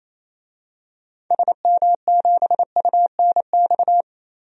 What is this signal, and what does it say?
Morse code, heard as a keyed tone, about 27 wpm, 700 Hz. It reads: SM7UNX
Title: DXCC-100 Call Signs - 6 Characters - Repeated 3x - 28wpm